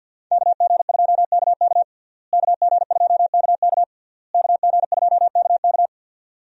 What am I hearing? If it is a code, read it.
Morse code, heard as a keyed tone, about 51 wpm, 700 Hz. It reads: KG2KK KG2KK KG2KK
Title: USA Call Signs - 5 Characters - Repeated 3x - 50wpm